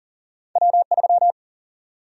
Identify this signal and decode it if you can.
Morse code, heard as a keyed tone, about 40 wpm, 700 Hz. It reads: W3
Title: USA Call Sign Prefixes - Repeated 2x - 40wpm